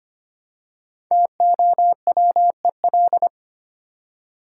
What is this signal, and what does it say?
Morse code, heard as a keyed tone, about 25 wpm, 700 Hz. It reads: TOWEL
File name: practice / 25wpm / Easy Words with Hints - 25wpm